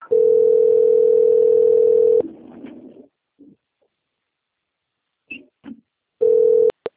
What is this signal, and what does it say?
Ringback tone (North America), 440 + 480 Hz.
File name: out- - - - - .51928